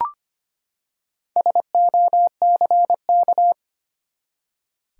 Morse code, heard as a keyed tone, about 25 wpm, 700 Hz. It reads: SOCK
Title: Four Letter Words - 25wpm